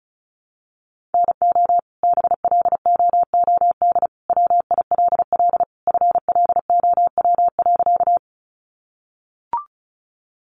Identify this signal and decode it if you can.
Morse code, heard as a keyed tone, about 34 wpm, 700 Hz. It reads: NO BLOOD WILL FLOW.
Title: Sentences from Top 1200 Words 35wpm